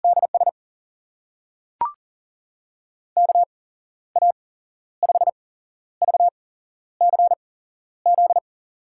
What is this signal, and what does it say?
Morse code, heard as a keyed tone, about 39 wpm, spaced as for 12 wpm, 700 Hz. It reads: DS KA5VCZ
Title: USA Call Signs - 8x Character Spacing - 40wpm